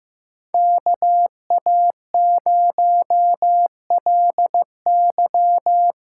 Morse code, heard as a keyed tone, about 15 wpm, 700 Hz. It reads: KA0LY
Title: USA Call Signs - 6 Characters - 8x Character Spacing - 15wpm